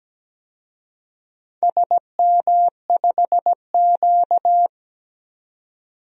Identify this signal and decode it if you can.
Morse code, heard as a keyed tone, about 17 wpm, 700 Hz. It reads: SM5Q
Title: DXCC-100 Call Signs - 4 Characters - 17wpm